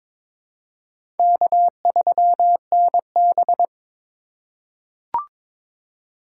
Morse code, heard as a keyed tone, about 22 wpm, 700 Hz. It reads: K3NB
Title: USA Call Signs - 4 Characters - 4x Character Spacing - 22wpm